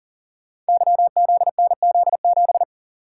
Morse code, heard as a keyed tone, about 38 wpm, 700 Hz. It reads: YZNZ7
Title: Code Groups - 5 Letter:Numbers - 40wpm